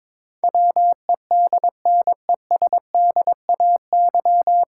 Morse code, heard as a keyed tone, about 21 wpm, 700 Hz. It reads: WEDNESDAY
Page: Morse Code Ninja practice 22wpm